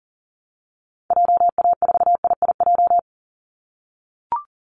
Morse code, heard as a keyed tone, about 40 wpm, 700 Hz. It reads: JA4IIJ
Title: Call Signs - 40wpm